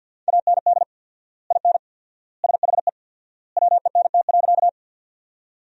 Morse code, heard as a keyed tone, about 50 wpm, 700 Hz. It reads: AND IN SHE WENT.
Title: 4x Word-Spacing - Sentences from Top 200 Words - 50wpm